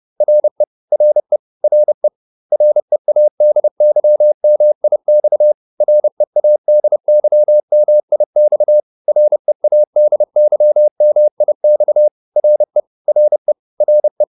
Morse code, heard as a keyed tone, about 30 wpm, 600 Hz. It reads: RE RE RE READYMIX READYMIX READYMIX RE RE RE